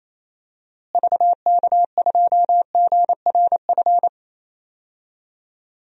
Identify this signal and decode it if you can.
Morse code, heard as a keyed tone, about 27 wpm, 700 Hz. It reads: VK2GRF
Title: DXCC-100 Call Signs - 6 Characters - Repeated 3x - 28wpm